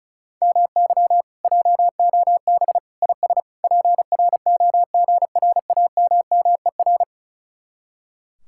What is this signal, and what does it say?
Morse code, heard as a keyed tone, about 43 wpm, 700 Hz. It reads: MY JOB IS PROGRAMMER
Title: Speed-Racing Common QSO Phrases Encore - 35wpm